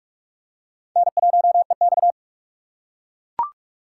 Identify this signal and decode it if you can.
Morse code, heard as a keyed tone, about 44 wpm, 700 Hz. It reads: N1EX